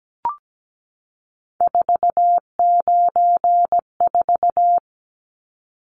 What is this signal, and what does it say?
Morse code, heard as a keyed tone, about 17 wpm, 700 Hz. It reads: 494
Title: Numbers - 3-Digit 17wpm